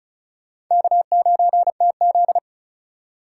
Morse code, heard as a keyed tone, about 44 wpm, 700 Hz. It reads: K9TZ